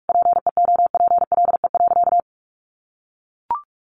Morse code, heard as a keyed tone, about 44 wpm, 700 Hz. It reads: PEOPLE.
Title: Sentences from Top 1500 Words 45wpm